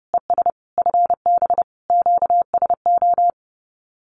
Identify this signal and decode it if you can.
Morse code, heard as a keyed tone, about 28 wpm, 700 Hz. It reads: ES FB QSO
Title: Common QSO Phrases Encore 30wpm